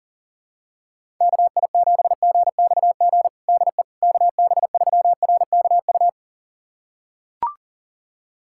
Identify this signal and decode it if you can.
Morse code, heard as a keyed tone, about 40 wpm, 700 Hz. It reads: KI7GXG DE KB3RKU
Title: USA Callsign DE Callsign 40wpm